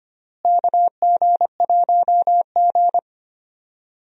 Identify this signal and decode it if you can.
Morse code, heard as a keyed tone, about 25 wpm, 700 Hz. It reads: KG1G
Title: USA Call Signs - 4 Characters - Repeated 2x - 25wpm